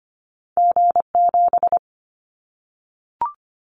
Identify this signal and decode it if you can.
Morse code, heard as a keyed tone, about 25 wpm, 700 Hz. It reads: G7
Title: Callsign Prefixes for Top 25 DXCC - Repeated 3x - 25wpm